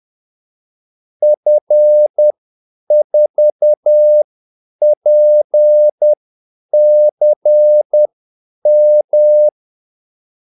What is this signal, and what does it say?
Morse code, heard as a keyed tone, about 10 wpm, spaced as for 8 wpm, 600 Hz. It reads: F4PCM
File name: Callsigns-10-8wpm